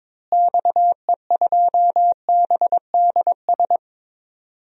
Morse code, heard as a keyed tone, about 21 wpm, 700 Hz. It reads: XE2BDS
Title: DXCC-100 Call Signs - 6 Characters - Repeated 3x - 22wpm